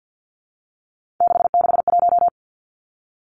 Morse code, heard as a keyed tone, about 50 wpm, 700 Hz. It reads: 661
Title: Three Challenging Letter-Number Combinations 50wpm